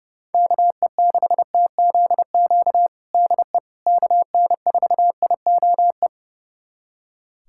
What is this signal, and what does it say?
Morse code, heard as a keyed tone, about 30 wpm, 700 Hz. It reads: KE6TZQ DE KN4IOE
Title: USA Callsign DE Callsign 30wpm